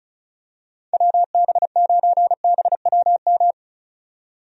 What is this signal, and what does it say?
Morse code, heard as a keyed tone, about 36 wpm, 700 Hz. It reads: WB9BWM
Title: USA Call Signs - 6 Characters - 35wpm